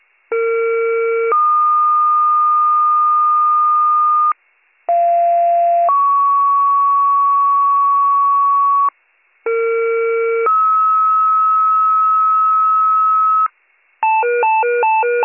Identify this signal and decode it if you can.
Two-tone page, heard as a one-second tone, then a three-second tone, about 460 then 1220 Hz, about 690 then 1120 Hz, about 460 then 1290 Hz. It is heard from 0.3 to 13.5 s.